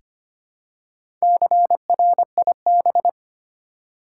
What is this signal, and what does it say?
Morse code, heard as a keyed tone, about 25 wpm, 700 Hz. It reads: CRIB
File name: practice / 25wpm / Easy Words with Hints - 25wpm